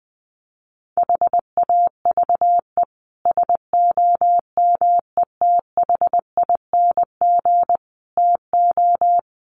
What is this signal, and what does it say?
Morse code, heard as a keyed tone, about 20 wpm, 700 Hz. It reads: HAVE SOMETHING TO
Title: Sets of 5 Words Encore 20wpm